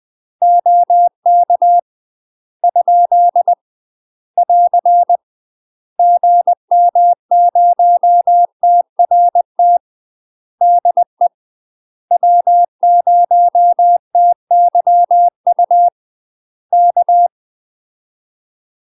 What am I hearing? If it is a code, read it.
Morse code, heard as a keyed tone, about 20 wpm, 700 Hz. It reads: OK ? <AR> GM0TRT DE W0TYU K
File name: UK MASTER QSOs TRAINING / 20WPM / UK MASTER QSOs 0050